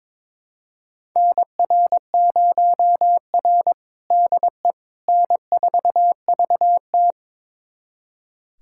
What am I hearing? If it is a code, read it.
Morse code, heard as a keyed tone, about 22 wpm, 700 Hz. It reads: NR0R DE N4VT
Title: USA Callsign DE Callsign - 4 Characters - 22wpm